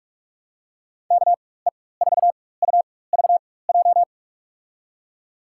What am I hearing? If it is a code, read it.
Morse code, heard as a keyed tone, about 44 wpm, spaced as for 21 wpm, 700 Hz. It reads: KE4UVJ
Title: USA Call Signs - 4x Character Spacing - 45wpm